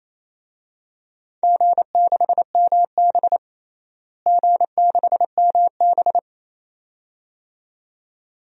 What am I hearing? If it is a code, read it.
Morse code, heard as a keyed tone, about 28 wpm, 700 Hz. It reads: G6MB G6MB
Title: DXCC-100 Call Signs - 4 Characters - Repeated 2x - 28wpm